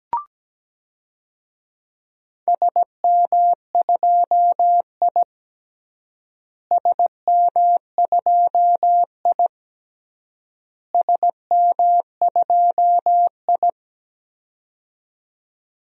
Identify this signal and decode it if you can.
Morse code, heard as a keyed tone, about 17 wpm, 700 Hz. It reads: SM2I SM2I SM2I
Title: DXCC-100 Call Signs - 4 Characters - Repeated 3x - 17wpm